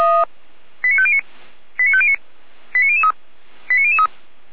Five-tone selective call (ZVEI), heard as quick runs of five tones, tones of about 70 ms. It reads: 89509 89509 89003 89003
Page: SDR/ZVEI